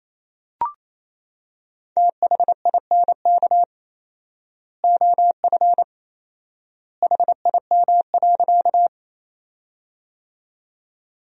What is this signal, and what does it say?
Morse code, heard as a keyed tone, about 28 wpm, 700 Hz. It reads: THINK OF HIM.
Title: 4x Word-Spacing - Sentences from Top 200 Words - 28wpm